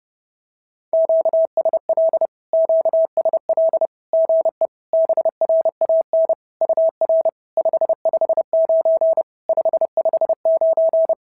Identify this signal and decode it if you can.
Morse code, heard as a keyed tone, about 30 wpm, 650 Hz. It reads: QSL QSL GE BRAN UR 559 559
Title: SOTA QSO Practice 30wpm